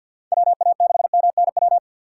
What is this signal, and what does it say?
Morse code, heard as a keyed tone, about 53 wpm, 700 Hz. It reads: WA6MNW
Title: USA Call Signs - 6 Characters - 4x Character Spacing - 50wpm